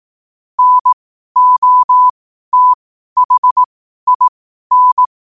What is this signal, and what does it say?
Morse code, heard as a keyed tone, about 18 wpm, spaced as for 12 wpm, 1000 Hz. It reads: NOTHIN
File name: qotdfiles / 12wpm